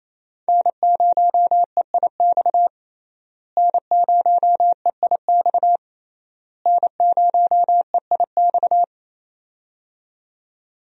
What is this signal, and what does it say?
Morse code, heard as a keyed tone, about 28 wpm, 700 Hz. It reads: N0EIX N0EIX N0EIX
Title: USA Call Signs - 5 Characters - Repeated 3x - 28wpm